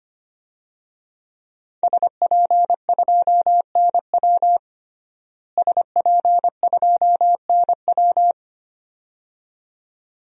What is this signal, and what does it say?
Morse code, heard as a keyed tone, about 25 wpm, 700 Hz. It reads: SP2NW SP2NW
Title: DXCC-100 Call Signs - 5 Characters - Repeated 2x - 25wpm